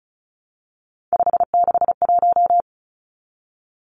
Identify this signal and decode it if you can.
Morse code, heard as a keyed tone, about 34 wpm, 700 Hz. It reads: 561
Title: Numbers - 3-Digit 35wpm